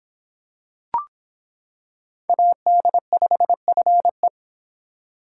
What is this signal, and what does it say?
Morse code, heard as a keyed tone, about 26 wpm, 700 Hz. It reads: AD5FE